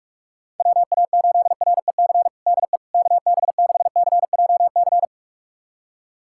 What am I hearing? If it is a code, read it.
Morse code, heard as a keyed tone, about 46 wpm, 700 Hz. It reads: WA8REC DE KB6CJC
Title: USA Callsign DE Callsign - 6 Characters - 45wpm